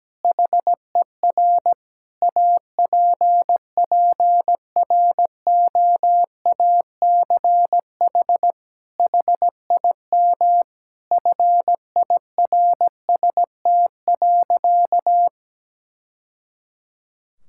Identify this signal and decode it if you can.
Morse code, heard as a keyed tone, about 17 wpm, 700 Hz. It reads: HER APPROACH HIM FIRST.
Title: Sentences from Top 1000 Words - Review Part 6 - 17wpm